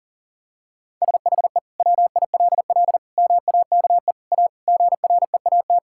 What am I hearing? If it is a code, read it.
Morse code, heard as a keyed tone, about 40 wpm, 700 Hz. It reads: SHE WILL MAKE A GREAT
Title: Sentences from Top 1000 Words - Review Part 6 - 40wpm